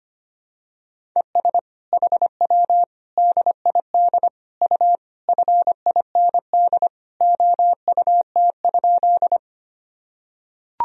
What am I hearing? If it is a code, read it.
Morse code, heard as a keyed tone, about 25 wpm, 700 Hz. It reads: ES HW DID U FIND OUT?
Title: Sentences from Top 100 Words and Abbreviations 25wpm